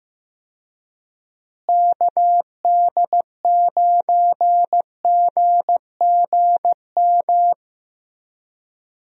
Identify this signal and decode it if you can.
Morse code, heard as a keyed tone, about 16 wpm, 700 Hz. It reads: KD9GGM
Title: USA Call Signs - 6 Characters - Repeated 3x - 15wpm